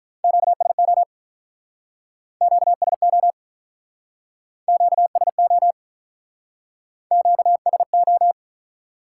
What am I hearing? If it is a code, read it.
Morse code, heard as a keyed tone, about 47 wpm, 700 Hz. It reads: QSO QSO QSO QSO